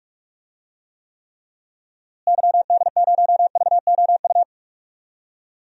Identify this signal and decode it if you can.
Morse code, heard as a keyed tone, about 46 wpm, 700 Hz. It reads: YD0VOU
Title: DXCC-100 Call Signs - 6 Characters - Repeated 3x - 45wpm